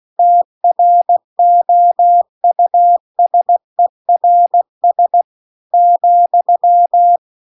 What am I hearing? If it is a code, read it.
Morse code, heard as a keyed tone, about 16 wpm, 700 Hz. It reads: TROUSERS ,